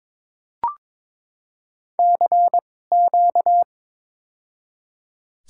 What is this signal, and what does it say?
Morse code, heard as a keyed tone, about 22 wpm, spaced as for 16 wpm, 700 Hz. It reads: CQ
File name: Top 90 QSO Elements - 2x Character Spacing - 22wpm